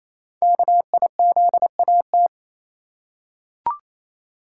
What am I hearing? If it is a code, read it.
Morse code, heard as a keyed tone, about 27 wpm, 700 Hz. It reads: KIZAT
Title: Code Groups - 5 Letters - 28wpm